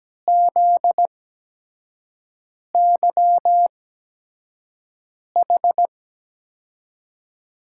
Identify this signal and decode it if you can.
Morse code, heard as a keyed tone, about 17 wpm, spaced as for 5 wpm, 700 Hz. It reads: ZYH
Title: USA Call Signs - 6 Characters - 8x Character Spacing - 17wpm